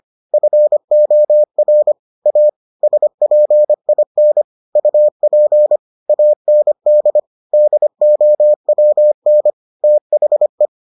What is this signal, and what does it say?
Morse code, heard as a keyed tone, about 25 wpm, 600 Hz. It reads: FOR A SPIN UP AND DOWN THE